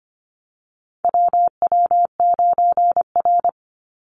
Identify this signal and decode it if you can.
Morse code, heard as a keyed tone, about 25 wpm, 700 Hz. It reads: WW9R
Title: Call Signs Repeated 3x - 25wpm